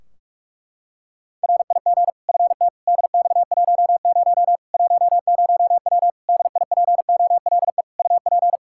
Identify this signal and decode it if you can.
Morse code, heard as a keyed tone, about 46 wpm, 700 Hz. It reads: RIG FT DX10 10W DIPOLE UP
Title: Practice QSO - Exchange 2 - Info - 45wpm